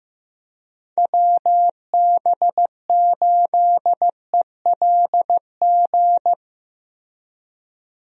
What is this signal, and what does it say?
Morse code, heard as a keyed tone, about 16 wpm, 700 Hz. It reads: WB8ELG
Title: USA Call Signs - 6 Characters - Repeated 2x - 15wpm